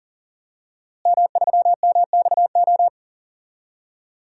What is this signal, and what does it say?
Morse code, heard as a keyed tone, about 40 wpm, 700 Hz. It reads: M3MXO
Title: DXCC-100 Call Signs - 5 Characters - Repeated 2x - 40wpm